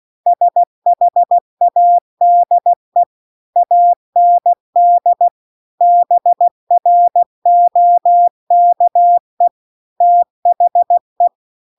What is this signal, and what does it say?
Morse code, heard as a keyed tone, about 16 wpm, 700 Hz. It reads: SHADE AND BROKE THE
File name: CW-The-Big-Twoheaded-River-2-16WPM